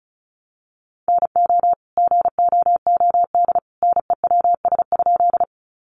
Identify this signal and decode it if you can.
Morse code, heard as a keyed tone, about 34 wpm, 700 Hz. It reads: NO GOOD NEWS?
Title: Sentences from Top 800 Words 35wpm